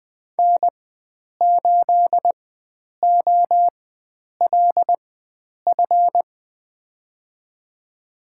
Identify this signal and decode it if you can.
Morse code, heard as a keyed tone, about 20 wpm, spaced as for 9 wpm, 700 Hz. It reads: N8OLF
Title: USA Call Signs - 4x Character Spacing - 20wpm